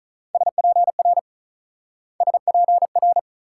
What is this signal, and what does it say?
Morse code, heard as a keyed tone, about 47 wpm, 700 Hz. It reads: SPR SPR